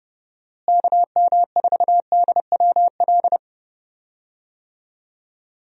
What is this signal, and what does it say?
Morse code, heard as a keyed tone, about 30 wpm, 700 Hz. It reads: KM4DWL
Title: USA Call Signs - 6 Characters - Repeated 2x - 30wpm